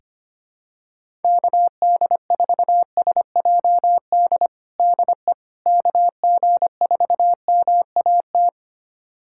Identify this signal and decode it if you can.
Morse code, heard as a keyed tone, about 25 wpm, 700 Hz. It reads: KD4SJD DE KG4MAT
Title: USA Callsign DE Callsign 25wpm